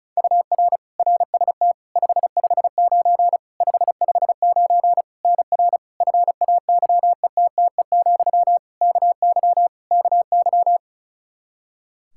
Morse code, heard as a keyed tone, about 36 wpm, 700 Hz. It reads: UR RST 559 559 NR FAYETTE, KY KY
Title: Practice QSO - Exchange 1 - Beginning - 35wpm